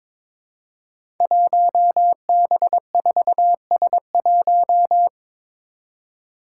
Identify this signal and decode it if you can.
Morse code, heard as a keyed tone, about 22 wpm, 700 Hz. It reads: 1B4S1
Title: Five Challenging Letter-Number Combinations 22wpm